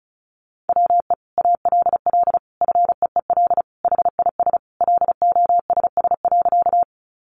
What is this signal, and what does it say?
Morse code, heard as a keyed tone, about 34 wpm, 700 Hz. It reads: WE ALL FEEL HIS LOSS.
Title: Sentences from Top 1400 Words 35wpm